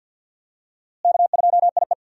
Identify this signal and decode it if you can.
Morse code, heard as a keyed tone, about 54 wpm, 700 Hz. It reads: K2IE
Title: USA Call Signs - 4 Characters - Repeated 3x - 50wpm